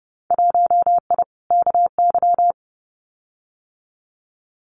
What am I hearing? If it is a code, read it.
Morse code, heard as a keyed tone, about 30 wpm, 700 Hz. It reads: 1I KY
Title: Winter Field Day Exchanges 30wpm